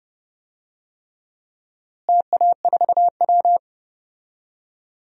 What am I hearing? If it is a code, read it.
Morse code, heard as a keyed tone, about 30 wpm, 700 Hz. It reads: TA4W
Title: DXCC-100 Call Signs - 4 Characters - Repeated 3x - 30wpm